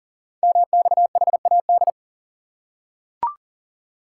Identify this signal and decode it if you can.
Morse code, heard as a keyed tone, about 39 wpm, 690 Hz. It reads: MXHAD
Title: Code Groups - 5 Letters - 40wpm